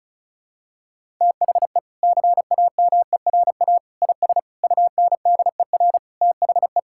Morse code, heard as a keyed tone, about 36 wpm, 700 Hz. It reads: THE CAMERA IS UNDER THE
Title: Sentences from Top 1000 Words - Review Part 5 - 35wpm